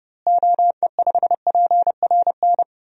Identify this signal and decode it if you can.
Morse code, heard as a keyed tone, about 32 wpm, 700 Hz. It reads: OE5PRN
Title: DXCC-100 Call Signs - 6 Characters - Repeated 3x - 30wpm